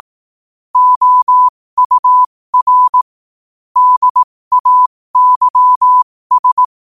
Morse code, heard as a keyed tone, about 18 wpm, spaced as for 15 wpm, 1000 Hz. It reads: OUR DAYS
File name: qotdfiles / 15wpm